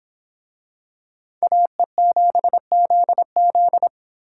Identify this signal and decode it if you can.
Morse code, heard as a keyed tone, about 26 wpm, 700 Hz. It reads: AE7ZZ